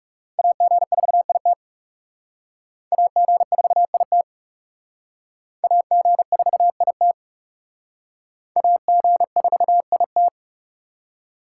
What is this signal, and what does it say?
Morse code, heard as a keyed tone, about 40 wpm, 700 Hz. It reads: AG4IT AG4IT AG4IT AG4IT